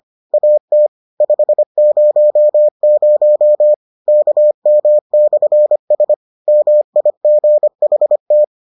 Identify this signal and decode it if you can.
Morse code, heard as a keyed tone, about 25 wpm, 600 Hz. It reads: AT 500 KM/S MIGHT